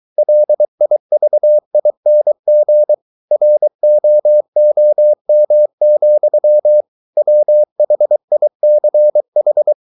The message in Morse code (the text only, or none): LIVING ROOM, WHICH